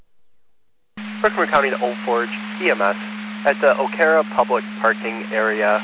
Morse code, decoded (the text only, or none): none